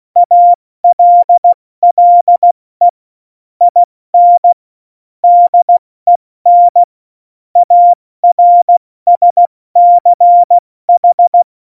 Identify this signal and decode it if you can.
Morse code, heard as a keyed tone, about 16 wpm, 700 Hz. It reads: ALLE IN DEN ARSCH